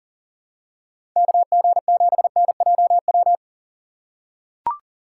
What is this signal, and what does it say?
Morse code, heard as a keyed tone, about 39 wpm, 700 Hz. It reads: KG7NJW
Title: USA Call Signs - 6 Characters - 2x Character Spacing - 40wpm